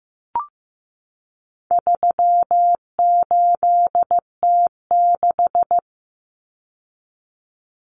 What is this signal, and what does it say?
Morse code, heard as a keyed tone, about 15 wpm, 700 Hz. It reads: 38T6
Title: Common Cut Numbers - 4-Digit 15wpm